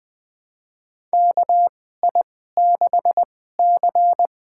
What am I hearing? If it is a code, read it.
Morse code, heard as a keyed tone, about 20 wpm, spaced as for 14 wpm, 700 Hz. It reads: KI6C
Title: USA Call Signs - 2x Character Spacing - 20wpm